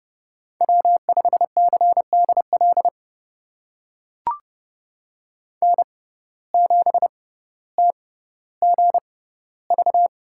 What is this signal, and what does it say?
Morse code, heard as a keyed tone, about 30 wpm, spaced as for 15 wpm, 700 Hz. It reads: W5CDL N7TGV
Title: USA Call Signs - 5 Characters - 6x Character Spacing - 30wpm